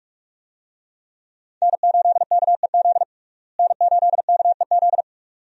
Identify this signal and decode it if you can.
Morse code, heard as a keyed tone, about 47 wpm, 700 Hz. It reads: N8KEZ N8KEZ
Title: USA Call Signs - 5 Characters - Repeated 2x - 45wpm